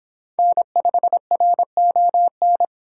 Morse code, heard as a keyed tone, about 25 wpm, 700 Hz. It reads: N5RON